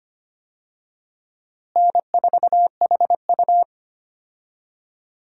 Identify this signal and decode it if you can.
Morse code, heard as a keyed tone, about 25 wpm, 700 Hz. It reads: N4HU
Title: USA Call Signs - 4 Characters - Repeated 3x - 25wpm